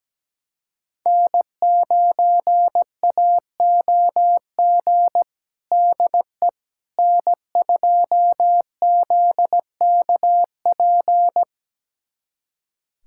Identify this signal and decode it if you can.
Morse code, heard as a keyed tone, about 17 wpm, 700 Hz. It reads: N9AOG DE N2ZKP
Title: USA Callsign DE Callsign - 5 Characters - 17wpm